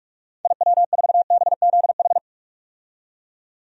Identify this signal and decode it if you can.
Morse code, heard as a keyed tone, about 43 wpm, 700 Hz. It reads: IW4BZH
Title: DXCC-100 Call Signs - 6 Characters - 45wpm